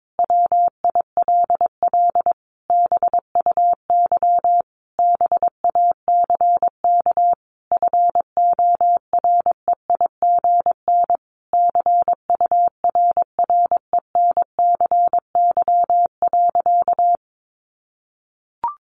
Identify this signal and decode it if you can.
Morse code, heard as a keyed tone, about 22 wpm, 700 Hz. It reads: WILL BUY BACK FOREIGN CURRENCY.
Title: Sentences from Top 1500 Words 22wpm